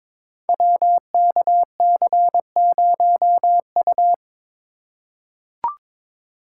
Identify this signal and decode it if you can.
Morse code, heard as a keyed tone, about 22 wpm, 700 Hz. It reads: WKC0U
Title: Code Groups - 5 Letter:Numbers - 22wpm